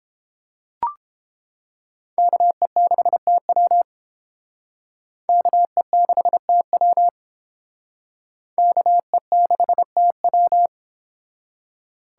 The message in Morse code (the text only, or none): KE6TW KE6TW KE6TW